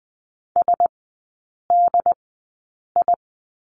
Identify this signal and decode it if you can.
Morse code, heard as a keyed tone, about 20 wpm, spaced as for 8 wpm, 690 Hz. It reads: SDI
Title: 081 Lesson 14e - Number 2 - Three Characters 20wpm